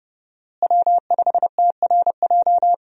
Morse code, heard as a keyed tone, about 31 wpm, 700 Hz. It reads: W5TRJ
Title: USA Call Signs - 5 Characters - Repeated 3x - 30wpm